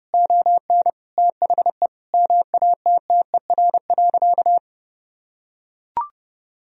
Morse code, heard as a keyed tone, about 32 wpm, 700 Hz. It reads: ON THE MATTER.
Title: Sentences from Top 1000 Words - Review Part 4 - 30wpm